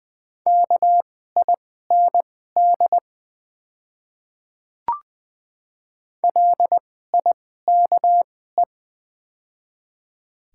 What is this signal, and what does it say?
Morse code, heard as a keyed tone, about 20 wpm, spaced as for 14 wpm, 700 Hz. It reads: KIND LIKE
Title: 2x Character Spacing - Top 300 Words - 20wpm